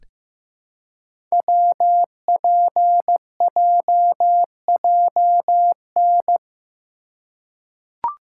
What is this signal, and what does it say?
Morse code, heard as a keyed tone, about 15 wpm, 700 Hz. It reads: WPJJN
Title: Code Groups - 5 Letters - 15wpm